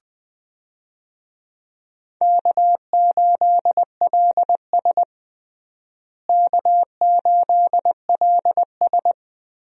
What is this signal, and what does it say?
Morse code, heard as a keyed tone, about 21 wpm, 700 Hz. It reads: K8LS K8LS
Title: USA Call Signs - 4 Characters - Repeated 2x - 20wpm